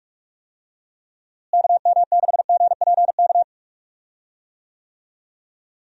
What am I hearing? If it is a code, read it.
Morse code, heard as a keyed tone, about 43 wpm, 700 Hz. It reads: KM6GPK
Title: USA Call Signs - 6 Characters - Repeated 2x - 45wpm